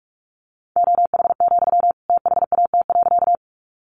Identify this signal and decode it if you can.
Morse code, heard as a keyed tone, about 44 wpm, 700 Hz. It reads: OH, THAT.
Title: Sentences from Top 500 Words 45wpm